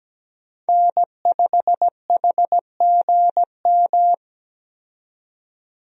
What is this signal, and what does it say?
Morse code, heard as a keyed tone, about 17 wpm, 700 Hz. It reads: N5HGM
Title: USA Call Signs - 5 Characters - 17wpm